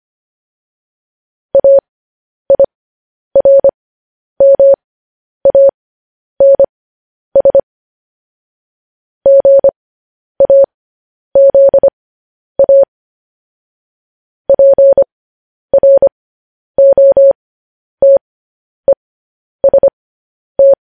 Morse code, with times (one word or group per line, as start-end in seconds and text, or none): 1.6-7.6 AIRMANS
9.3-12.8 GAZA
14.5-20.7 PROTEST